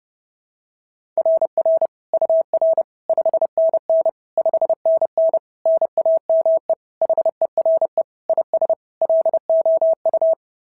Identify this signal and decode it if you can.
Morse code, heard as a keyed tone, about 32 wpm, 690 Hz. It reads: RR UR 5NN 5NN NAME HERE IS LOU
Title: SOTA QSO Practice 30wpm